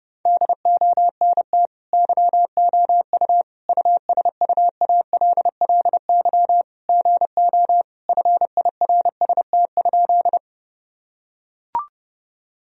Morse code, heard as a keyed tone, about 30 wpm, 700 Hz. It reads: DONT YOU USUALLY GO FIRST?
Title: Sentences from Top 1000 Words - Review Part 3 - 30wpm